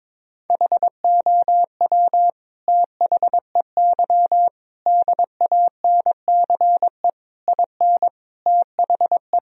Morse code, heard as a keyed tone, about 22 wpm, 700 Hz. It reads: HOW THEY DANCE IN THE
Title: Sentences from Top 1000 Words - Review Part 4 - 22wpm